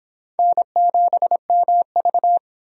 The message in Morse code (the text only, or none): N7MV